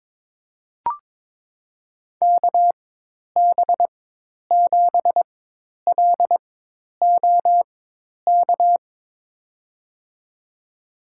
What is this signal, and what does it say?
Morse code, heard as a keyed tone, about 21 wpm, spaced as for 10 wpm, 700 Hz. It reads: KB7LOK
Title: USA Call Signs - 6 Characters - 4x Character Spacing - 22wpm